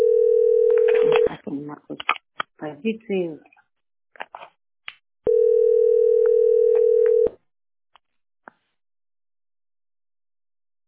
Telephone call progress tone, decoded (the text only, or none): ringback tone